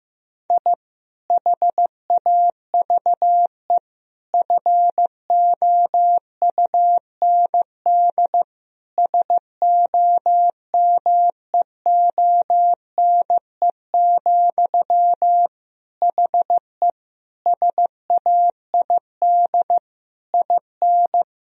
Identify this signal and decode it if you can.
Morse code, heard as a keyed tone, about 15 wpm, 700 Hz. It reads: I HAVE FOUND SOMEONE, HE SAID IN
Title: Sentences from Top 1000 Words - Review Part 1 - 15wpm